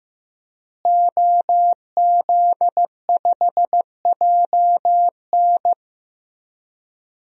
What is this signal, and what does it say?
Morse code, heard as a keyed tone, about 15 wpm, 700 Hz. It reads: OZ5JN
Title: DXCC-100 Call Signs - 5 Characters - Repeated 3x - 15wpm